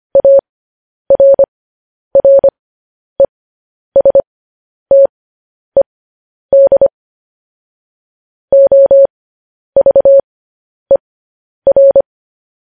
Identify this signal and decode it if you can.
Morse code, heard as a keyed tone, about 25 wpm, spaced as for 10 wpm, 550 Hz. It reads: ARRESTED OVER
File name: podcast / News headlines-1x-25x10